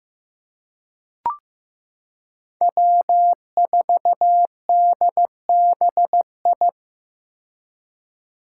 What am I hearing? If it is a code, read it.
Morse code, heard as a keyed tone, about 15 wpm, 700 Hz. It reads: W4DBI
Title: Five Challenging Letter-Number Combinations 15wpm